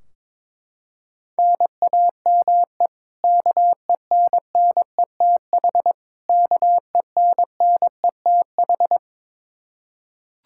Morse code, heard as a keyed tone, about 21 wpm, 700 Hz. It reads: NAME KENNETH KENNETH
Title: Practice QSO - Exchange 1 - Beginning - 22wpm